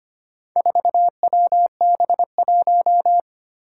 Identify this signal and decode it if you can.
Morse code, heard as a keyed tone, about 25 wpm, 700 Hz. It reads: 4WB1
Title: Four Challenging Letter-Number Combinations 25wpm